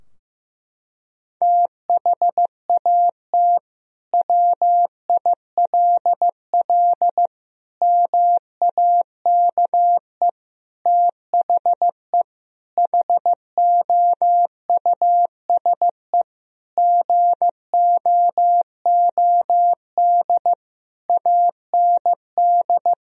Morse code, heard as a keyed tone, about 15 wpm, 700 Hz. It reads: THAT WILL MAKE THE HOUSE GOOD AND
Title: Sentences from Top 1000 Words - Review Part 3 - 15wpm